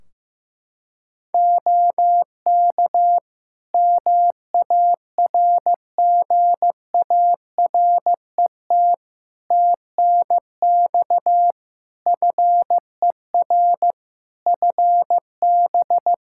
Morse code, heard as a keyed tone, about 15 wpm, 700 Hz. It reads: OK MARGARET TNX FER FB
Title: Practice QSO - Exchange 4 - Ending - 15wpm